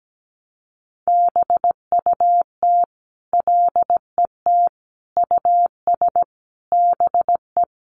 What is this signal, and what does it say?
Morse code, heard as a keyed tone, about 17 wpm, 700 Hz. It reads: BUT LET US BE
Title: Sentences from Top 1100 Words 17wpm